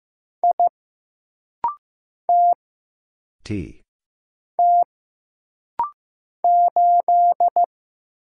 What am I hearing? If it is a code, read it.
Morse code, heard as a keyed tone, about 15 wpm, 700 Hz. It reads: I T T 8
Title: Single Letter-Number Alt - Rapid-Fire 15wpm